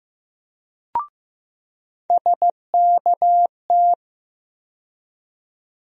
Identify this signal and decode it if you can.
Morse code, heard as a keyed tone, about 15 wpm, 700 Hz. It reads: SKT
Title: USA Call Sign Postfixes - 3 Letters 15wpm